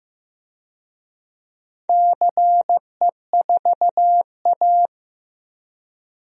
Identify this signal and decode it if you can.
Morse code, heard as a keyed tone, about 15 wpm, 700 Hz. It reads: CE4A
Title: DXCC-100 Call Signs - 4 Characters - 15wpm